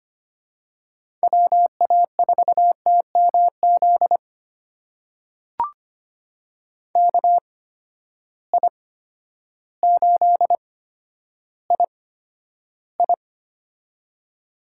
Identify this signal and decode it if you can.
Morse code, heard as a keyed tone, about 25 wpm, spaced as for 11 wpm, 700 Hz. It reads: WA4TMZ KI8II
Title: USA Call Signs - 8x Character Spacing - 25wpm